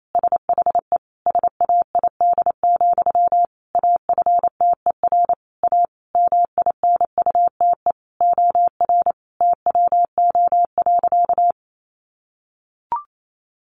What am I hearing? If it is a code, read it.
Morse code, heard as a keyed tone, about 28 wpm, 700 Hz. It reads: SHE SAID, AFTER A MINUTE OR TWO.
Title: Sentences from Top 1400 Words 28wpm